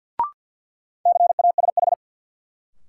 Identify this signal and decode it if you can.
Morse code, heard as a keyed tone, about 52 wpm, 700 Hz. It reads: CASH
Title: Sentences from Top 700 Words - New Word Drill - 50wpm